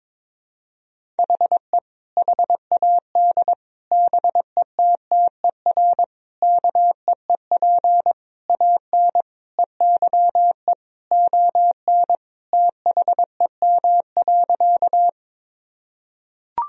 Morse code, heard as a keyed tone, about 22 wpm, 700 Hz. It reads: HE HAD BETTER KEEP AN EYE ON THEM.
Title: Sentences from Top 1000 Words - Review Part 1 - 22wpm